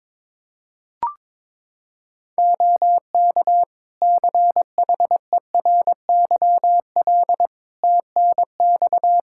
Morse code, heard as a keyed tone, about 22 wpm, 700 Hz. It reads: OK CHERYL TNX
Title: Practice QSO - Exchange 4 - Ending - 22wpm